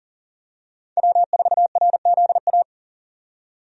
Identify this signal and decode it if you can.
Morse code, heard as a keyed tone, about 40 wpm, 700 Hz. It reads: W4RZA
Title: USA Call Signs - 5 Characters - 40wpm